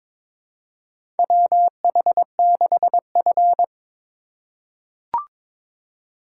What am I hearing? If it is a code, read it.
Morse code, heard as a keyed tone, about 22 wpm, 700 Hz. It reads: WH6F
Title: USA Call Signs - 4 Characters - 8x Character Spacing - 22wpm